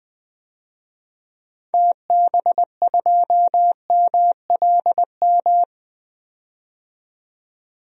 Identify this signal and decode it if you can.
Morse code, heard as a keyed tone, about 20 wpm, 700 Hz. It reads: TB2MLM
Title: DXCC-100 Call Signs - 6 Characters - 20wpm